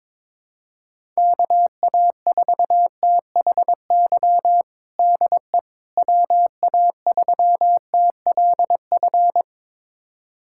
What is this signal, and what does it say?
Morse code, heard as a keyed tone, about 22 wpm, 700 Hz. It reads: KA4THY DE WA3TLF
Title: USA Callsign DE Callsign 22wpm